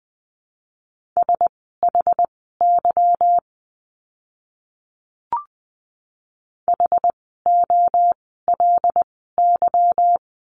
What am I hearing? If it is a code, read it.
Morse code, heard as a keyed tone, about 19 wpm, spaced as for 14 wpm, 700 Hz. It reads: SHY HOLY
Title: 146 Lesson 24f - Letter y - Words 20wpm